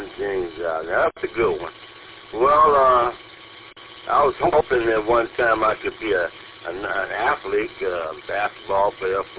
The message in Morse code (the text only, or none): none